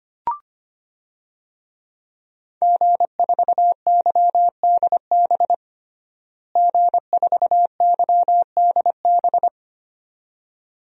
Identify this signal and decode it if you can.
Morse code, heard as a keyed tone, about 25 wpm, 700 Hz. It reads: G4YDB G4YDB
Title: DXCC-100 Call Signs - 5 Characters - Repeated 2x - 25wpm